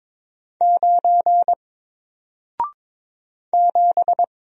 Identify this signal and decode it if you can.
Morse code, heard as a keyed tone, about 22 wpm, 700 Hz. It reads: 9 7
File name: Single-Letter-Number-22wpm